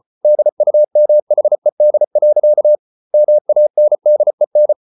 Morse code, heard as a keyed tone, about 34 wpm, 600 Hz. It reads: DUMHED. MANDEN
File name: Ferie QRQ 34wpm